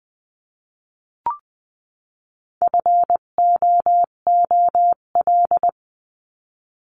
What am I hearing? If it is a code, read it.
Morse code, heard as a keyed tone, about 20 wpm, 700 Hz. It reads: FOOL
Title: 258 Lesson 42c - Words 16-20wpm